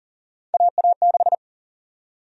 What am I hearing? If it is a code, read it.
Morse code, heard as a keyed tone, about 40 wpm, 700 Hz. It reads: AA6
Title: USA Call Sign Prefixes - Repeated 2x - 40wpm